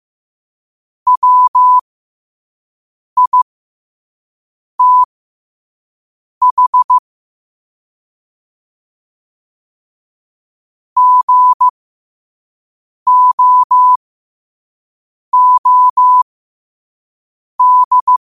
Morse code, heard as a keyed tone, about 15 wpm, spaced as for 5 wpm, 1000 Hz. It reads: WITH GOOD
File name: qotdfiles / 5wpm